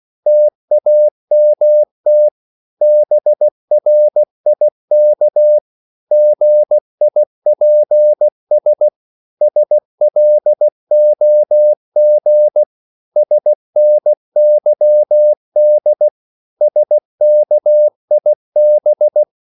Morse code, heard as a keyed tone, about 16 wpm, 600 Hz. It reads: TAMT BRIK GIPS SLOG SNYD SKIB
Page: Korte danske ord 16wpm